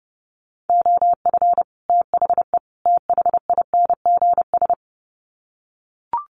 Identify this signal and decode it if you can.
Morse code, heard as a keyed tone, about 30 wpm, 700 Hz. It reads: OF THE THINGS
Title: Sets of 3 Words 30wpm